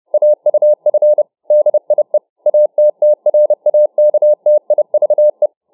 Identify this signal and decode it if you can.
Morse code, heard as a keyed tone, about 30 wpm, 600 Hz. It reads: AUF DIE ATTRAKTIVE